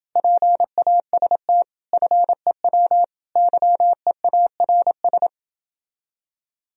Morse code, heard as a keyed tone, about 29 wpm, 700 Hz. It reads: PAST FEW YEARS